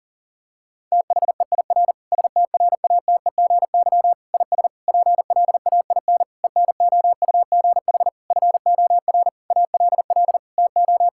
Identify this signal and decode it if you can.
Morse code, heard as a keyed tone, about 40 wpm, 700 Hz. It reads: THEIR STRATEGY IS PLAIN ENOUGH FOR ALL TO
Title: Sentences from Top 1000 Words - Review Part 5 - 40wpm